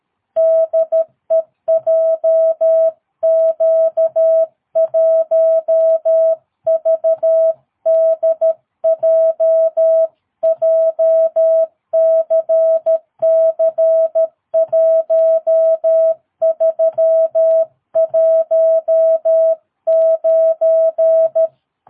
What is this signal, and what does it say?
Morse code, heard as a keyed tone, about 13 wpm, 650 Hz. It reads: DE JQ1VDJ JCC1319
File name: jh8cbh / 2022cw